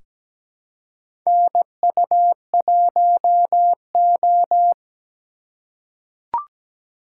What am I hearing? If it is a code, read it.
Morse code, heard as a keyed tone, about 17 wpm, 700 Hz. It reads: NU1O
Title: K1USN SST QSO Practice - 17wpm